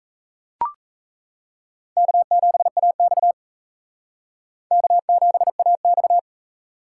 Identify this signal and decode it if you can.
Morse code, heard as a keyed tone, about 35 wpm, 700 Hz. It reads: K7AX K7AX